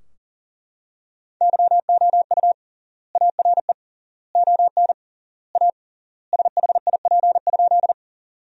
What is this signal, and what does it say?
Morse code, heard as a keyed tone, about 40 wpm, 700 Hz. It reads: YOU ARE ON A SHIP?
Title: 3x Word-Spacing - Sentences from Top 400 Words - 40wpm